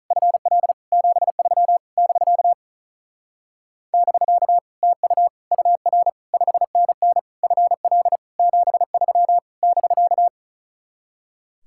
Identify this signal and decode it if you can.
Morse code, heard as a keyed tone, about 42 wpm, 700 Hz. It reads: FL 73 <BK> <BK> TU UR 5NN FL 73 <BK>
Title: Speed-Racing Common QSO Phrases - 35wpm